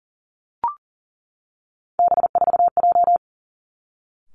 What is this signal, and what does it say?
Morse code, heard as a keyed tone, about 41 wpm, 700 Hz. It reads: B4J
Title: Three Challenging Letter-Number Combinations 40wpm